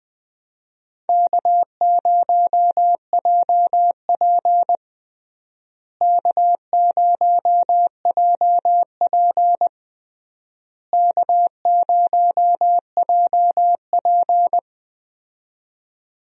Morse code, heard as a keyed tone, about 20 wpm, 700 Hz. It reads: K0JP K0JP K0JP
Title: USA Call Signs - 4 Characters - Repeated 3x - 20wpm